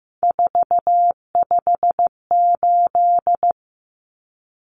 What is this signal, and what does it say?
Morse code, heard as a keyed tone, about 15 wpm, 700 Hz. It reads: 458
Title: Common Cut Numbers - 3-Digit 15wpm